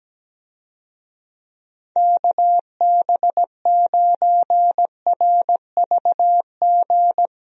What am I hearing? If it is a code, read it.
Morse code, heard as a keyed tone, about 17 wpm, 700 Hz. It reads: KB9RVG
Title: USA Call Signs - 6 Characters - Repeated 2x - 17wpm